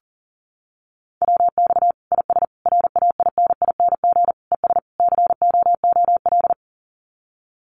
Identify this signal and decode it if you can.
Morse code, heard as a keyed tone, about 37 wpm, 700 Hz. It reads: WX IS RAINING ES COOL
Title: Common QSO Phrases 40wpm